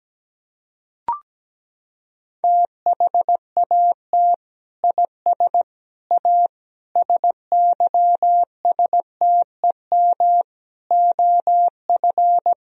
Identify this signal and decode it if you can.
Morse code, heard as a keyed tone, about 17 wpm, 700 Hz. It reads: THAT IS A SYSTEM OF
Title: Sentences from Top 1000 Words - Review Part 3 - 17wpm